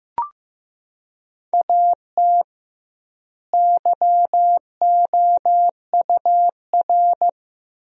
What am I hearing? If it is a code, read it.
Morse code, heard as a keyed tone, about 15 wpm, 700 Hz. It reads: AT YOUR
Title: 2x Word-Spacing - Sentences from Top 200 Words - 15wpm